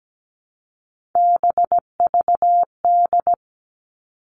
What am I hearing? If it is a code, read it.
Morse code, heard as a keyed tone, about 17 wpm, 700 Hz. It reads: BVD
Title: Three Challenging Letter-Number Combinations 17wpm